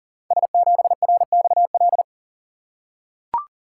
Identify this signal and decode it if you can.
Morse code, heard as a keyed tone, about 40 wpm, 700 Hz. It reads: S7RXL
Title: Code Groups - 5 Letter:Numbers - 40wpm